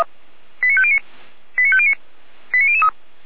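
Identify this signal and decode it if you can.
Five-tone selective call (ZVEI), heard as quick runs of five tones, tones of about 70 ms. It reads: 89509 89509 89003
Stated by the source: ZVEI five-tone call 89099 received on SDR by feuerwehr-orlamuende.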